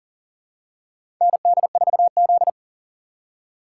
Morse code, heard as a keyed tone, about 40 wpm, 700 Hz. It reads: ND4Z
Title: USA Call Signs - 4 Characters - 4x Character Spacing - 40wpm